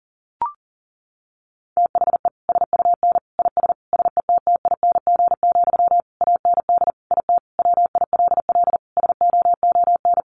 Morse code, heard as a keyed tone, about 40 wpm, 700 Hz. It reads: THE SUN IS SETTING, AND IT WILL SOON
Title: Sentences from Top 1300 Words 40wpm